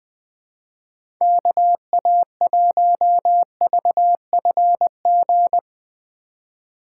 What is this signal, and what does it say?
Morse code, heard as a keyed tone, about 21 wpm, 700 Hz. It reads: KA1VFG
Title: USA Call Signs - 6 Characters - 2x Character Spacing - 20wpm